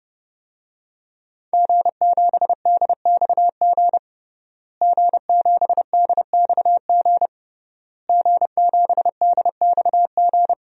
Morse code, heard as a keyed tone, about 31 wpm, 700 Hz. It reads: G7DXG G7DXG G7DXG
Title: DXCC-100 Call Signs - 5 Characters - Repeated 3x - 30wpm